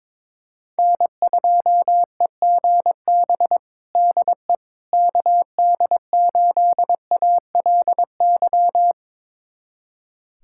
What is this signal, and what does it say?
Morse code, heard as a keyed tone, about 22 wpm, 700 Hz. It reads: N2EGB DE KD8ALY
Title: USA Callsign DE Callsign 22wpm